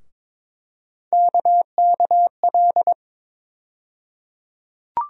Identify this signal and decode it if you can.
Morse code, heard as a keyed tone, about 22 wpm, 700 Hz. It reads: KKL
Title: USA Call Sign Postfixes - 3 Letters 22wpm